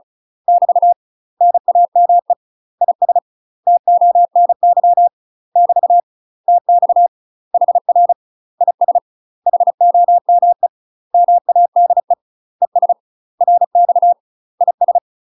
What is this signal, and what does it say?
Morse code, heard as a keyed tone, about 34 wpm, 700 Hz. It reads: <BT> NAME IS TONY <BT> TX HR IS HOME MADE ES RX IS
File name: UK MASTER QSOs TRAINING / 35WPM / UK MASTER QSOs 0043